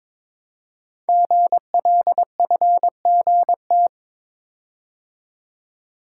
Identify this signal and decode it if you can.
Morse code, heard as a keyed tone, about 22 wpm, 700 Hz. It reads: GLFGT
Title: Code Groups - 5 Letters - 22wpm